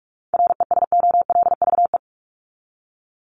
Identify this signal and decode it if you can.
Morse code, heard as a keyed tone, about 44 wpm, 700 Hz. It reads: RESOLVE
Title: Top 1751-2000 Words 45wpm